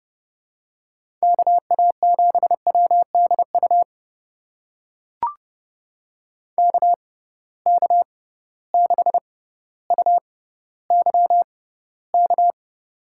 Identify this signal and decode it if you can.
Morse code, heard as a keyed tone, about 30 wpm, spaced as for 15 wpm, 700 Hz. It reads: KA7WDU KK6UYK
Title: USA Call Signs - 6 Characters - 6x Character Spacing - 30wpm